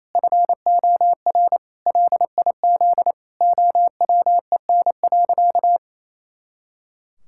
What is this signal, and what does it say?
Morse code, heard as a keyed tone, about 27 wpm, 700 Hz. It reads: FOR LIZ OWEN.
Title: Pangram Challenge v2 - 28wpm